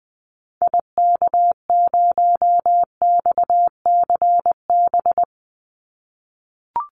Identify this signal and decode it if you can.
Morse code, heard as a keyed tone, about 20 wpm, 700 Hz. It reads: IK0XCB
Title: Call Signs - 20wpm